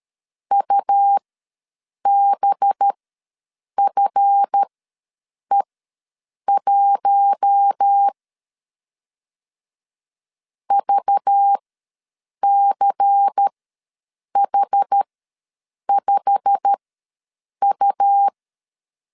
Morse code, heard as a keyed tone, about 13 wpm, spaced as for 7 wpm, 800 Hz. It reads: UBFE1 VCH5U
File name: LEZIONI / 10 SIGNS / 10D COMA ws 12wpm 3618